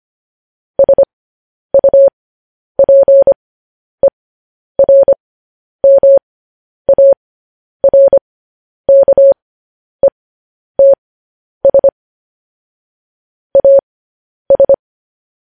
Morse code, heard as a keyed tone, about 25 wpm, spaced as for 10 wpm, 550 Hz. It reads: SUPERMARKETS AS
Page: news headlines in morse code at 10 wpm